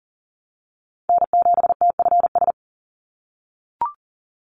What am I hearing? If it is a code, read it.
Morse code, heard as a keyed tone, about 40 wpm, 700 Hz. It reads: N7TFS
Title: USA Call Signs 40wpm